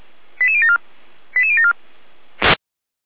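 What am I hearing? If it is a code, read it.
Five-tone selective call (ZVEI), heard as quick runs of five tones, tones of about 70 ms. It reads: 88074 88074
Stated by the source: Received on SDR.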